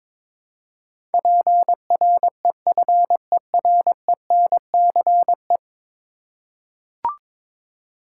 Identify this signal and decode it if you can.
Morse code, heard as a keyed tone, about 22 wpm, 700 Hz. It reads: PREFERENCE
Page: morse-code-ninja-practice-22wpm